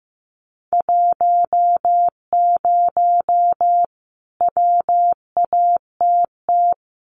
Morse code, heard as a keyed tone, about 15 wpm, 700 Hz. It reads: 10 WATT
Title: Common QSO Phrases Encore 15wpm